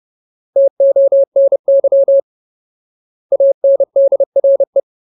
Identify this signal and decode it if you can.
Morse code, heard as a keyed tone, about 30 wpm, 550 Hz. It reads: TONY ANDRE
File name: Ultimate Word List / 30WPM Ultimate